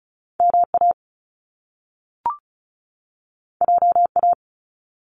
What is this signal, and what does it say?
Morse code, heard as a keyed tone, about 36 wpm, 700 Hz. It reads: MA JA
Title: Top Two Letter Combinations in Words 35wpm